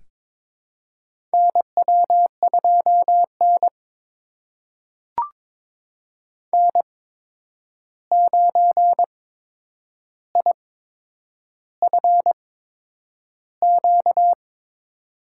Morse code, heard as a keyed tone, about 22 wpm, spaced as for 9 wpm, 700 Hz. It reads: NW2N N9IFQ
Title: USA Call Signs - 8x Character Spacing - 22wpm